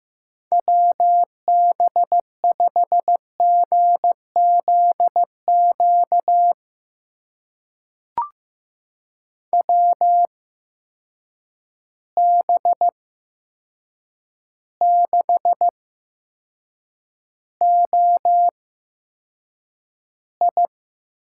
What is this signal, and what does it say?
Morse code, heard as a keyed tone, about 15 wpm, spaced as for 7 wpm, 700 Hz. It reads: WB5GZQ WB6OI
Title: USA Call Signs - 6 Characters - 8x Character Spacing - 15wpm